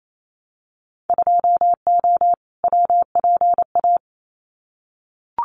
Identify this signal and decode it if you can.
Morse code, heard as a keyed tone, about 28 wpm, 700 Hz. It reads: 2O WPA
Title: Winter Field Day Exchanges 28wpm